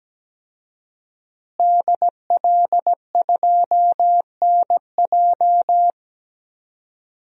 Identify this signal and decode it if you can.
Morse code, heard as a keyed tone, about 17 wpm, 700 Hz. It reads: DL2NJ
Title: DXCC-100 Call Signs - 5 Characters - Repeated 2x - 17wpm